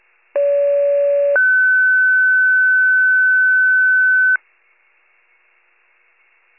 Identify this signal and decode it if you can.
Two-tone page, heard as a one-second tone, then a three-second tone, about 570 then 1500 Hz. It is heard from 0.3 to 4.4 s.